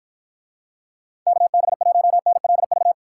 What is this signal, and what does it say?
Morse code, heard as a keyed tone, about 51 wpm, 700 Hz. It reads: KB1NLV